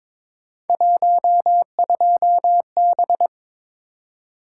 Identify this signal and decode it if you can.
Morse code, heard as a keyed tone, about 22 wpm, 700 Hz. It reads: 12B
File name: practice / 22wpm / Three Letter-Number Combinations - 22wpm